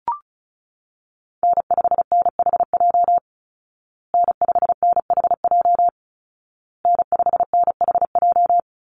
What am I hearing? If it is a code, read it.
Morse code, heard as a keyed tone, about 34 wpm, 700 Hz. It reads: N5NHJ N5NHJ N5NHJ
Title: Call Signs Repeated 2x - 35wpm